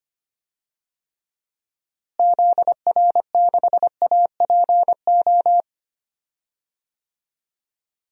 Morse code, heard as a keyed tone, about 25 wpm, 700 Hz. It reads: ZR6APO
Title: DXCC-100 Call Signs - 6 Characters - Repeated 3x - 25wpm